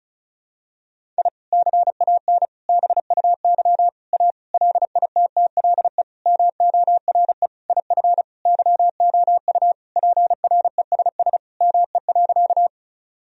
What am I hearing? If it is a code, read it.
Morse code, heard as a keyed tone, about 36 wpm, 700 Hz. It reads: I CAN BUY A LITTLE MORE IF YOU PRESS ME.
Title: Sentences from Top 1000 Words - Review Part 1 - 35wpm